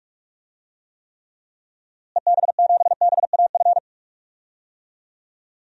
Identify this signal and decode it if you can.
Morse code, heard as a keyed tone, about 46 wpm, 700 Hz. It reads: EB7BAF
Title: DXCC-100 Call Signs - 6 Characters - Repeated 3x - 45wpm